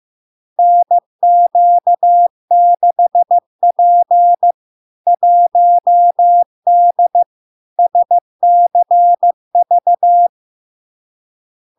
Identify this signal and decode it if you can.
Morse code, heard as a keyed tone, about 15 wpm, 700 Hz. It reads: NQ6P 1D SCV
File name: LARK-FD-2022-Log-in-CW-15wpm-700-Hz